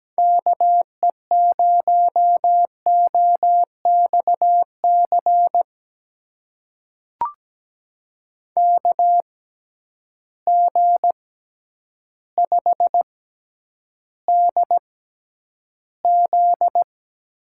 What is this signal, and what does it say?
Morse code, heard as a keyed tone, about 17 wpm, spaced as for 9 wpm, 700 Hz. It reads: KE0OXC KG5DZ
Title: USA Call Signs - 6 Characters - 6x Character Spacing - 17wpm